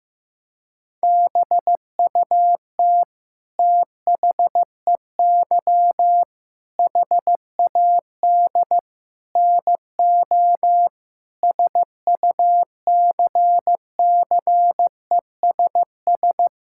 Morse code, heard as a keyed tone, about 15 wpm, 700 Hz. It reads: BUT THEY HAD NO SUCCESS
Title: Sentences from Top 1000 Words - Review Part 6 - 15wpm